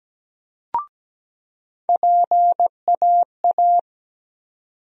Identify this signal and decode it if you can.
Morse code, heard as a keyed tone, about 17 wpm, 700 Hz. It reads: PAA
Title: USA Call Sign Postfixes - 3 Letters 17wpm